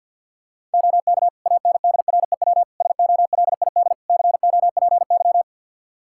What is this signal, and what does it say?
Morse code, heard as a keyed tone, about 48 wpm, 700 Hz. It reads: OK ANDREW SOLID COPY
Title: Practice QSO - Exchange 3 - More Info - 50wpm